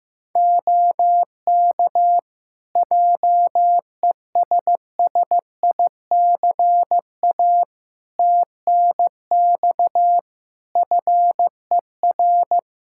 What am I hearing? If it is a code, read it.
Morse code, heard as a keyed tone, about 15 wpm, 700 Hz. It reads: OK JESSICA TNX FER
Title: Practice QSO - Exchange 4 - Ending - 15wpm